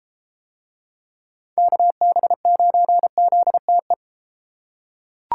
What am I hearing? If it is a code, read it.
Morse code, heard as a keyed tone, about 32 wpm, 700 Hz. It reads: KB9ZTE